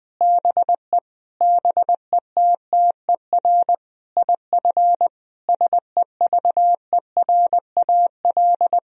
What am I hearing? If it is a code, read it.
Morse code, heard as a keyed tone, about 20 wpm, 700 Hz. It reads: BE BETTER IF SEVERAL
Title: Sentences from Top 1000 Words - Review Part 2 - 20wpm